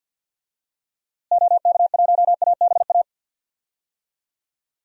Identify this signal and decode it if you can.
Morse code, heard as a keyed tone, about 51 wpm, 700 Hz. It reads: OK1ABA
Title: DXCC-100 Call Signs - 6 Characters - Repeated 3x - 50wpm